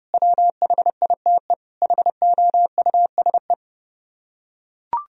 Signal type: Morse code

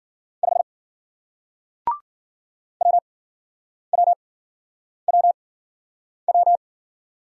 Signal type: Morse code

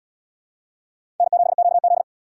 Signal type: Morse code

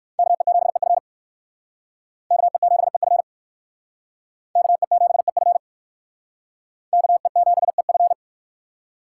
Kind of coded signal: Morse code

